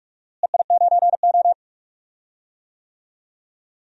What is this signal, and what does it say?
Morse code, heard as a keyed tone, about 46 wpm, 700 Hz. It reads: EI9O
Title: DXCC-100 Call Signs - 4 Characters - Repeated 2x - 45wpm